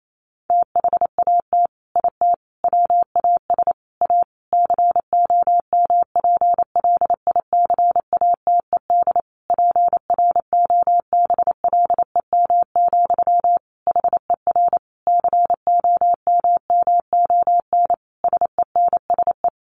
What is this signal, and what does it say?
Morse code, heard as a keyed tone, about 28 wpm, 700 Hz. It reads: THAT IT WAS A COMPLICATED PROBLEM, HER COMMON SENSE
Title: Sentences from Top 1200 Words 28wpm